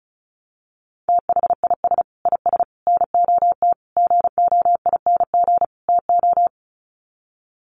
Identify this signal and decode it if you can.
Morse code, heard as a keyed tone, about 34 wpm, 700 Hz. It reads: THIS IS NOT GOING TO
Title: Sets of 5 Words Encore2 35wpm